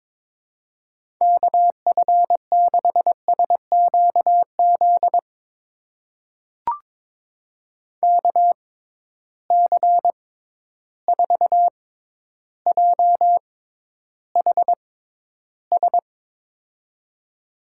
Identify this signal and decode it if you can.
Morse code, heard as a keyed tone, about 22 wpm, spaced as for 11 wpm, 700 Hz. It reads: KF6SQZ KC4JHS
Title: USA Call Signs - 6 Characters - 6x Character Spacing - 22wpm